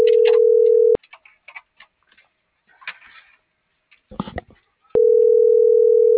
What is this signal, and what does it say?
Ringback tone (North America), 440 + 480 Hz.